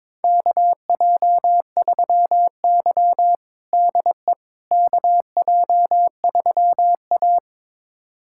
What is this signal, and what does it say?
Morse code, heard as a keyed tone, about 21 wpm, 700 Hz. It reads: KJ3Y DE KJ3A
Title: USA Callsign DE Callsign - 4 Characters - 22wpm